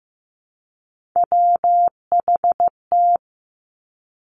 Morse code, heard as a keyed tone, about 15 wpm, 700 Hz. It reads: WHT
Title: Three Challenging Letter-Number Combinations 15wpm